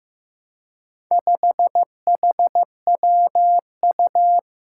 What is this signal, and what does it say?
Morse code, heard as a keyed tone, about 15 wpm, 700 Hz. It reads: 5HWU
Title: Four Challenging Letter-Number Combinations 15wpm